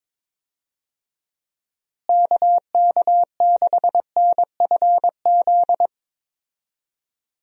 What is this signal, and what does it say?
Morse code, heard as a keyed tone, about 21 wpm, 700 Hz. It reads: KK6NFZ
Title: USA Call Signs - 6 Characters - Repeated 2x - 22wpm